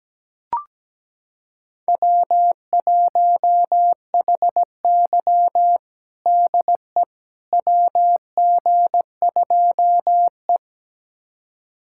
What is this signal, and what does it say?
Morse code, heard as a keyed tone, about 17 wpm, 700 Hz. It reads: W1HY DE WG2E
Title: USA Callsign DE Callsign - 4 Characters - 17wpm